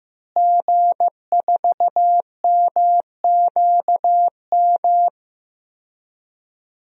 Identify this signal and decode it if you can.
Morse code, heard as a keyed tone, about 16 wpm, 700 Hz. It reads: G4MQM
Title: DXCC-100 Call Signs - 5 Characters - Repeated 3x - 15wpm